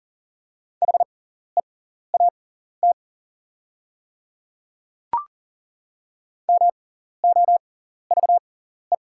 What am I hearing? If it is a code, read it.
Morse code, heard as a keyed tone, about 38 wpm, spaced as for 14 wpm, 700 Hz. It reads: HEAT MOVE
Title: 6x Character Spacing - Top 500 Words - 40wpm